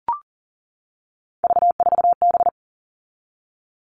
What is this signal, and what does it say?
Morse code, heard as a keyed tone, about 40 wpm, 700 Hz. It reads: V4B
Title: All Cut Numbers - 3-Digit 40wpm